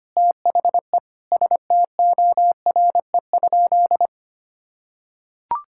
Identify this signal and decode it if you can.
Morse code, heard as a keyed tone, about 25 wpm, 700 Hz. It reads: THE STORE?
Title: Sentences from Top 1000 Words - Review Part 5 - 25wpm